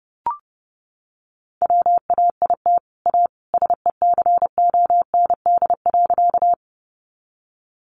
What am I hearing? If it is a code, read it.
Morse code, heard as a keyed tone, about 28 wpm, 700 Hz. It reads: WAIT A SECOND.
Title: Sentences from Top 500 Words 30wpm